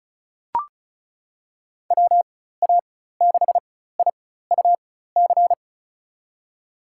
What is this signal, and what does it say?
Morse code, heard as a keyed tone, about 36 wpm, spaced as for 16 wpm, 700 Hz. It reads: WA6IUC
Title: USA Call Signs - 6 Characters - 4x Character Spacing - 35wpm